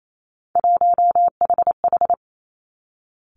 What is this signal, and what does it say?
Morse code, heard as a keyed tone, about 28 wpm, 700 Hz. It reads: 1HH
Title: Three Challenging Letter-Number Combinations 28wpm